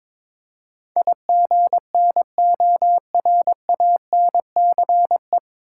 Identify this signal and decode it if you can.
Morse code, heard as a keyed tone, about 22 wpm, 700 Hz. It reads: IGNORANCE
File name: 9 Letter Words - First 3 Letters Given - 22wpm